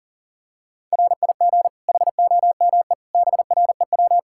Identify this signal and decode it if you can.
Morse code, heard as a keyed tone, about 40 wpm, 700 Hz. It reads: RIG HOME BREW
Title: Practice QSO - Exchange 2 - Info - 40wpm